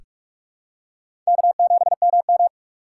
Morse code, heard as a keyed tone, about 44 wpm, 690 Hz. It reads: K7MM
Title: USA Call Signs - 4 Characters - 2x Character Spacing - 45wpm